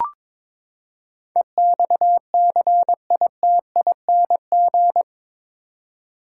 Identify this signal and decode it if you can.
Morse code, heard as a keyed tone, about 22 wpm, 700 Hz. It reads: EXCITING
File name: practice / 22wpm / Eight Letter Words - 22wpm